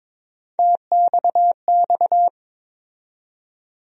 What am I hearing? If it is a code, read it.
Morse code, heard as a keyed tone, about 22 wpm, 700 Hz. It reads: TXX
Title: USA Call Sign Postfixes - 3 Letters 22wpm